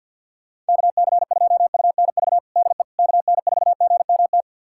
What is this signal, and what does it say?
Morse code, heard as a keyed tone, about 53 wpm, 700 Hz. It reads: KC2UNV DE KN4GMT
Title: USA Callsign DE Callsign - 6 Characters - 50wpm